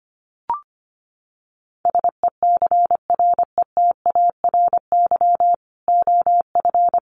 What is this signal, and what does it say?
Morse code, heard as a keyed tone, about 25 wpm, 700 Hz. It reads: SECRETARY OF
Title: Sentences from Top 1300 Words 25wpm